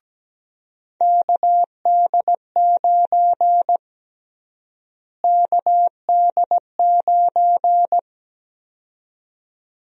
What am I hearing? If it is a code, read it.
Morse code, heard as a keyed tone, about 17 wpm, 700 Hz. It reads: KD9 KD9
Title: USA Call Sign Prefixes - Repeated 2x - 17wpm